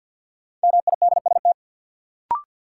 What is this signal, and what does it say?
Morse code, heard as a keyed tone, about 52 wpm, 700 Hz. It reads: MIDST
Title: Sets of 4 Words - New Word Drill - 50wpm